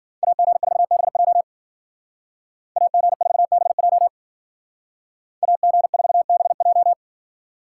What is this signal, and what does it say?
Morse code, heard as a keyed tone, about 52 wpm, 700 Hz. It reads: AG4BJ AG4BJ AG4BJ